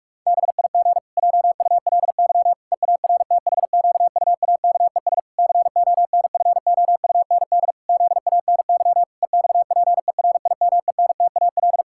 Morse code, heard as a keyed tone, about 46 wpm, 700 Hz. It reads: BIG JULY EARTHQUAKES CONFOUND ZANY EXPERIMENTAL
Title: Pangram Challenge v2 - 45wpm